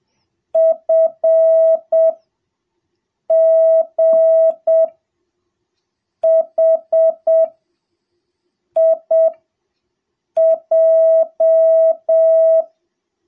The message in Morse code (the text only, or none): FGHIJ